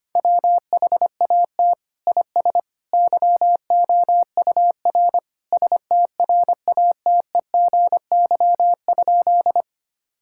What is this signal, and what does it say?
Morse code, heard as a keyed tone, about 25 wpm, 700 Hz. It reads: WHAT IS YOUR STRATEGY?
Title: Sentences from Top 1000 Words - Review Part 5 - 25wpm